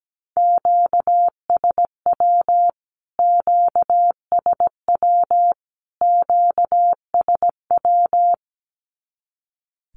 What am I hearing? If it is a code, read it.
Morse code, heard as a keyed tone, about 17 wpm, 700 Hz. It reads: QSW QSW QSW
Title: Q-Codes Repeated 3x 17wpm